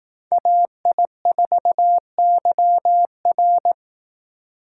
Morse code, heard as a keyed tone, about 18 wpm, 700 Hz. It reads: AI4YR
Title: Speed Racing - USA Call Signs 15wpm